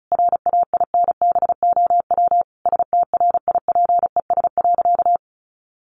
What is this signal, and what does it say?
Morse code, heard as a keyed tone, about 34 wpm, 700 Hz. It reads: RAINBOW STRIPES.
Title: Additional Easy Phrases 35wpm